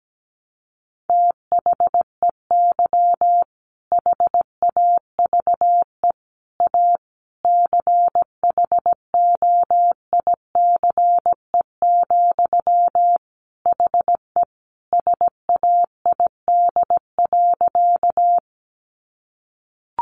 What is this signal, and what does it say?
Morse code, heard as a keyed tone, about 17 wpm, 700 Hz. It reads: THEY HAVE A CHOICE, HE SAID.
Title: Sentences from Top 900 Words 17wpm